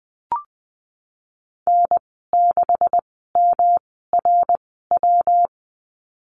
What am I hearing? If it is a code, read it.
Morse code, heard as a keyed tone, about 20 wpm, spaced as for 15 wpm, 700 Hz. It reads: N6MRW
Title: 179 Lesson 29g - Slash - Call Signs 20wpm